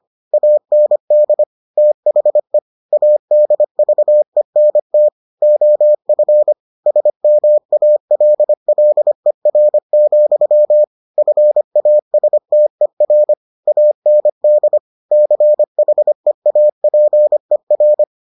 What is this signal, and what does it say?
Morse code, heard as a keyed tone, about 25 wpm, 600 Hz. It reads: AND THE ADVENT OF SMALLER, FASTER AND CHEAPER